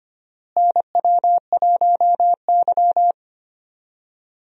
Morse code, heard as a keyed tone, about 25 wpm, 700 Hz. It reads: NW1Y
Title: USA Call Signs - 4 Characters - Repeated 2x - 25wpm